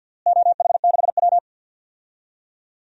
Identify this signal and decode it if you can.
Morse code, heard as a keyed tone, about 46 wpm, 700 Hz. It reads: OH6W
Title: DXCC-100 Call Signs - 4 Characters - Repeated 2x - 50wpm